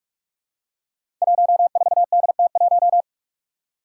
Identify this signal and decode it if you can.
Morse code, heard as a keyed tone, about 44 wpm, 700 Hz. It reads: 14DT1
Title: Five Challenging Letter-Number Combinations 45wpm